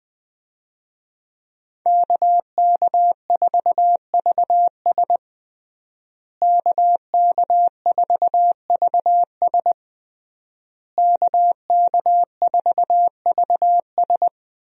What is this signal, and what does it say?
Morse code, heard as a keyed tone, about 20 wpm, 700 Hz. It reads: KK4VS KK4VS KK4VS
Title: USA Call Signs - 5 Characters - Repeated 3x - 20wpm